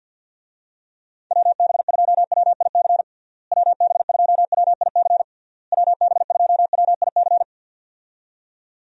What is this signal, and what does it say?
Morse code, heard as a keyed tone, about 53 wpm, 700 Hz. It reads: WB2WIC WB2WIC WB2WIC
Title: USA Call Signs - 6 Characters - Repeated 3x - 50wpm